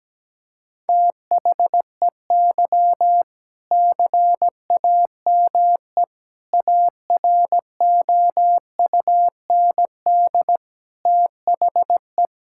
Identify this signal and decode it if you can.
Morse code, heard as a keyed tone, about 17 wpm, 700 Hz. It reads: THEY CAME AROUND THE
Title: Sentences from Top 1600 Words 17wpm